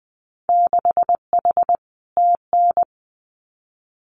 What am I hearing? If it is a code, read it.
Morse code, heard as a keyed tone, about 20 wpm, 700 Hz. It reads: 6H TN
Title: Winter Field Day Exchanges Encore 20wpm